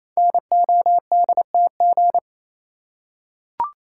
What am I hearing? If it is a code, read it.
Morse code, heard as a keyed tone, about 28 wpm, 700 Hz. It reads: NODTG
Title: Code Groups - 5 Letters - 28wpm